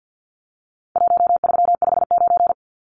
Morse code, heard as a keyed tone, about 49 wpm, 700 Hz. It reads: 1359
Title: Numbers - Contest Style - 4-Digit 50wpm